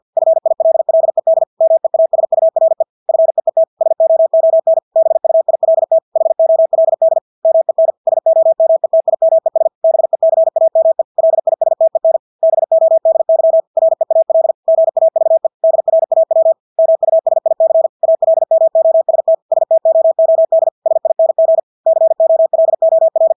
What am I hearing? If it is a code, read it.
Morse code, heard as a keyed tone, about 50 wpm, 650 Hz. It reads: FILLED MEASURE FEET SOON BUILT HOLD MEN SOMETIMES BECAME LISTEN BODY REAL GAVE DRAW MUSIC ALMOST STOOD SING COLOR
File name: Engelske ord 50 wpm